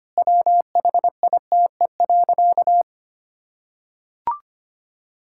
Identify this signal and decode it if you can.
Morse code, heard as a keyed tone, about 25 wpm, 700 Hz. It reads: WHITE.
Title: Pangram Challenge v2 - 25wpm